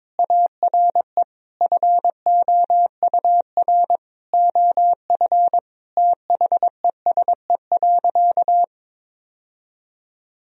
Morse code, heard as a keyed tone, about 22 wpm, 700 Hz. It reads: ARE FOUR OF THESE.
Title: Sentences from Top 1000 Words - Review Part 1 - 22wpm